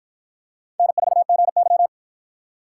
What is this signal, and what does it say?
Morse code, heard as a keyed tone, about 48 wpm, 700 Hz. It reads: N4GY